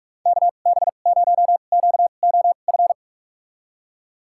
Morse code, heard as a keyed tone, about 46 wpm, spaced as for 33 wpm, 700 Hz. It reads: KB0QOF
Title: USA Call Signs - 6 Characters - 2x Character Spacing - 45wpm